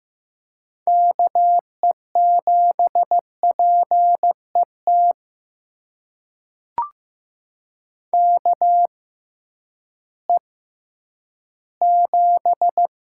Morse code, heard as a keyed tone, about 15 wpm, spaced as for 10 wpm, 700 Hz. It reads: KE7PET KE7
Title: USA Call Signs - 6 Characters - 6x Character Spacing - 15wpm